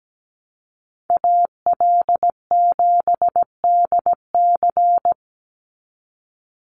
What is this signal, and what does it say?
Morse code, heard as a keyed tone, about 17 wpm, 700 Hz. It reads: AL7DC
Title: USA Call Signs 17wpm